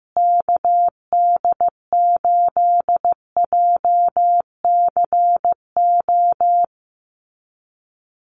Morse code, heard as a keyed tone, about 15 wpm, 700 Hz. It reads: KD8JCO
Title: USA Call Signs Repeated 3x 15wpm